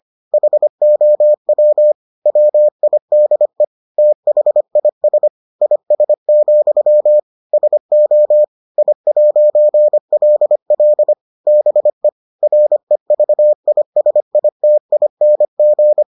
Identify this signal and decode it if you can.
Morse code, heard as a keyed tone, about 25 wpm, 600 Hz. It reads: HOW WIDE THIS IS, SO I'LL BE REVISITING